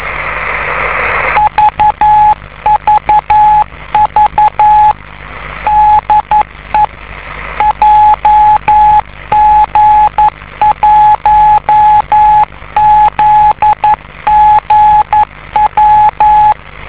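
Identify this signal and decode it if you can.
Morse code, heard as a keyed tone, about 11 wpm, 850 Hz, with radio noise. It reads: VVV DE JG1ZGW
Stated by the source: 50 MHz beacon located in Tokyo, Japan.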